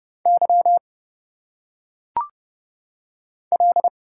Morse code, heard as a keyed tone, about 30 wpm, 700 Hz. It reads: Y L
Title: Single Letters - Phonetic Pronunciation 30wpm